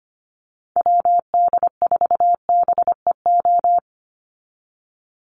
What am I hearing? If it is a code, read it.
Morse code, heard as a keyed tone, about 25 wpm, 700 Hz. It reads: WD4BEO
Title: USA Call Signs 25wpm